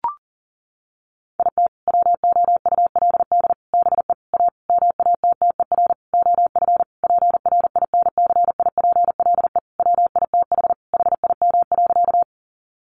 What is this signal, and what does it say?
Morse code, heard as a keyed tone, about 40 wpm, 700 Hz. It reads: IT WOULD BE A MATTER OF PRINCIPLE WITH HIM.
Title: Sentences from Top 1400 Words 40wpm